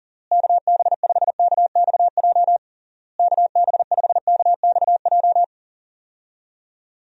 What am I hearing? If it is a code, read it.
Morse code, heard as a keyed tone, about 37 wpm, 700 Hz. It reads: KB5KXJ KB5KXJ
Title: USA Call Signs - 6 Characters - Repeated 3x - 40wpm